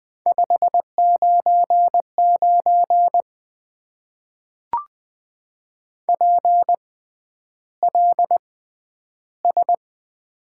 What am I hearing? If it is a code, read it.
Morse code, heard as a keyed tone, about 20 wpm, spaced as for 10 wpm, 700 Hz. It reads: 599 PLS
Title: Top 90 QSO Elements - 6x Character Spacing - 20wpm